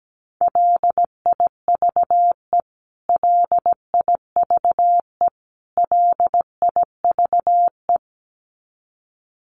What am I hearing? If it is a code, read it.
Morse code, heard as a keyed tone, about 17 wpm, 700 Hz. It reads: LIVE LIVE LIVE
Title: Top 500 Words Repeated 17wpm